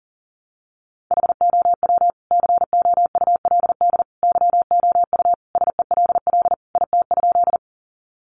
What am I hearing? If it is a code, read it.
Morse code, heard as a keyed tone, about 37 wpm, 700 Hz. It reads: HOW COULD YOU SELL IT?
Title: Sentences from Top 1100 Words 40wpm